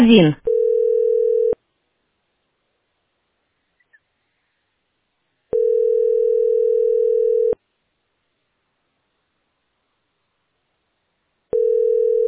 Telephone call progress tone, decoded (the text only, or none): ringback tone